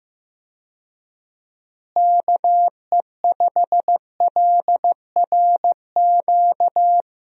Morse code, heard as a keyed tone, about 15 wpm, 700 Hz. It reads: KE5LRQ
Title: USA Call Signs - 6 Characters - Repeated 2x - 15wpm